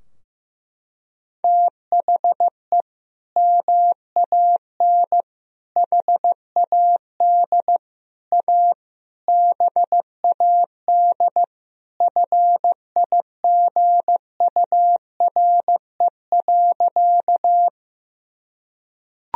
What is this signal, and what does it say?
Morse code, heard as a keyed tone, about 15 wpm, 700 Hz. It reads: THE MAN HAD A BAD FIGURE.
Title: Sentences from Top 1000 Words - Review Part 2 - 15wpm